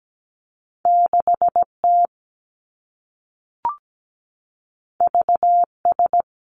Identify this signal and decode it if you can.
Morse code, heard as a keyed tone, about 17 wpm, 700 Hz. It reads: 6T VS
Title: Two Challenging Letter-Number Combinations 17wpm